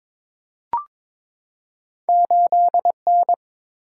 Morse code, heard as a keyed tone, about 22 wpm, 700 Hz. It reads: 8N
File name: practice / 22wpm / Two Letter-Number Combinations - 22wpm